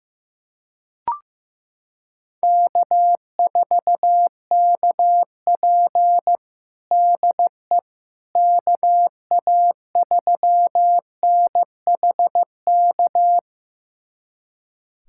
Morse code, heard as a keyed tone, about 16 wpm, 700 Hz. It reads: K4KP DE KA3NHK
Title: USA Callsign DE Callsign 15wpm